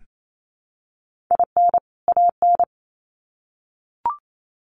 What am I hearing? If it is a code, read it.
Morse code, heard as a keyed tone, about 28 wpm, 700 Hz. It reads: IN AN
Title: Sets of 2 Words 28wpm